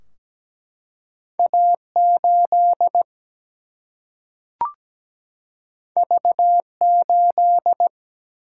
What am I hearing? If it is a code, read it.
Morse code, heard as a keyed tone, about 17 wpm, 700 Hz. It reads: A 8 V 8
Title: Single-Letter Single-Number 17wpm